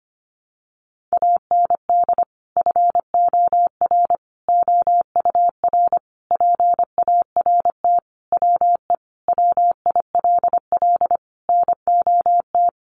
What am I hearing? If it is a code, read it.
Morse code, heard as a keyed tone, about 25 wpm, 700 Hz. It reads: AND FOR OUR PART WE WILL NOT
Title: Sentences from Top 1100 Words 25wpm